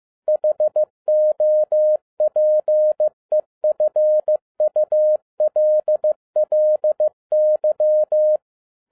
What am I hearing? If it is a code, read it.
Morse code, heard as a keyed tone, about 15 wpm, 610 Hz. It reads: HOPEFULLY